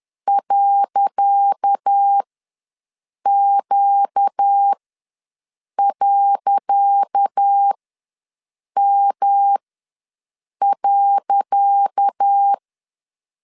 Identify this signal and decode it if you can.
Morse code, heard as a keyed tone, about 10 wpm, spaced as for 6 wpm, 800 Hz. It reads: .Q.M.
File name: LEZIONI / 10 SIGNS / 10C DOT ws 10wpm 4213